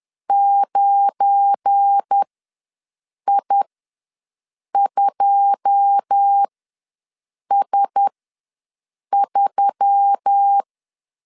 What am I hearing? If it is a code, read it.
Morse code, heard as a keyed tone, about 10 wpm, spaced as for 6 wpm, 800 Hz. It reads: 9I2S3
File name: LEZIONI / 9 28 ws 10wpm 4124